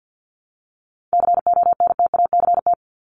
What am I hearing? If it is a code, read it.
Morse code, heard as a keyed tone, about 50 wpm, 700 Hz. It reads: CONTACT
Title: Top 751-1000 Words 50wpm